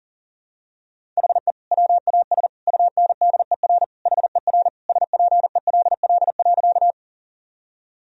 Morse code, heard as a keyed tone, about 40 wpm, 700 Hz. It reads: HE WAS UNDER HER SPELL.
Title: Sentences from Top 1000 Words - Review Part 1 - 40wpm